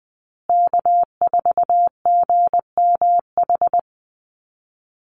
Morse code, heard as a keyed tone, about 20 wpm, 700 Hz. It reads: K4GMH
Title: Call Signs Repeated 2x - 20wpm